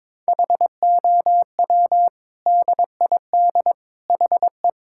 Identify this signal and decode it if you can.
Morse code, heard as a keyed tone, about 22 wpm, 700 Hz. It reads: HOW DID HE
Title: Sentences from Top 1600 Words 22wpm